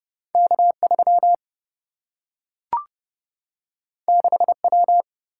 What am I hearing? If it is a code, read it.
Morse code, heard as a keyed tone, about 30 wpm, 700 Hz. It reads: K 3 6 W
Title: Single-Letter Single-Number 30wpm